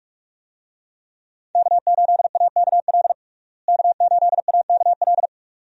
Morse code, heard as a keyed tone, about 46 wpm, 700 Hz. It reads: K8AKL K8AKL
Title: USA Call Signs - 5 Characters - Repeated 2x - 45wpm